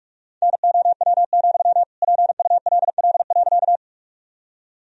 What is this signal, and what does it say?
Morse code, heard as a keyed tone, about 41 wpm, 700 Hz. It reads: NOW, PULL.
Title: Sentences from Top 1000 Words - Review Part 2 - 45wpm